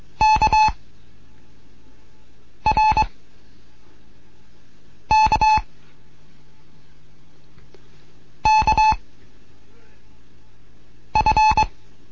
Morse code, heard as a keyed tone, about 22 wpm, 860 Hz. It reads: K R K K F